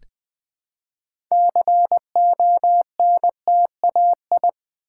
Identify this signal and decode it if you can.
Morse code, heard as a keyed tone, about 20 wpm, 700 Hz. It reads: CONTAI
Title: Build A Word - 3 Letters At A Time - 9-Letter Words - 20wpm